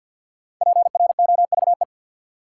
Morse code, heard as a keyed tone, about 52 wpm, 700 Hz. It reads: PROVE
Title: Sentences from Top 1600 Words - New Word Drill Alt - 50wpm